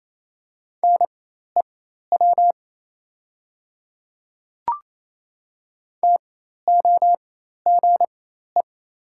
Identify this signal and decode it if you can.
Morse code, heard as a keyed tone, about 27 wpm, spaced as for 13 wpm, 700 Hz. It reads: NEW TOGE
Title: 4x Character Spacing - Top 300 Words - 28wpm